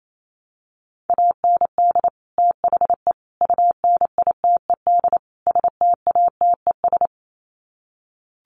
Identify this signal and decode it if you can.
Morse code, heard as a keyed tone, about 28 wpm, 700 Hz. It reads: AND THE UNITED STATES
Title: Sets of 4 Words 28wpm